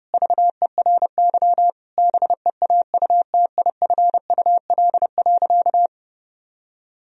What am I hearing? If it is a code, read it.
Morse code, heard as a keyed tone, about 30 wpm, 700 Hz. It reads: VERY BEAUTIFUL.
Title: Sentences from Top 1000 Words - Review Part 1 - 30wpm